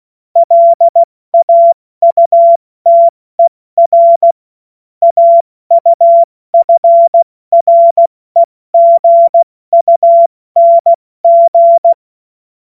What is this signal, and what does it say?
Morse code, heard as a keyed tone, about 16 wpm, spaced as for 14 wpm, 680 Hz. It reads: LAUTER AUFREGUNG
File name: Morsekurs / Witze / Witz 17. Mai 2022 70 F80